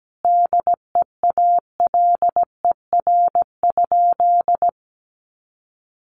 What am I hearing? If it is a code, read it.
Morse code, heard as a keyed tone, about 17 wpm, 700 Hz. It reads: DEALER?
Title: Sentences from Top 1100 Words 17wpm